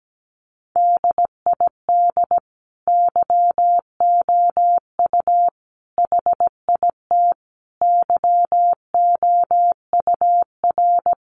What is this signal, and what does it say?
Morse code, heard as a keyed tone, about 17 wpm, 700 Hz. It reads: DID YOU HIT YOUR
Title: Sentences from Top 1400 Words 17wpm